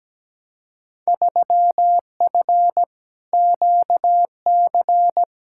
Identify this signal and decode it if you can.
Morse code, heard as a keyed tone, about 17 wpm, 700 Hz. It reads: 3F QC
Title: Field Day Exchanges - 17wpm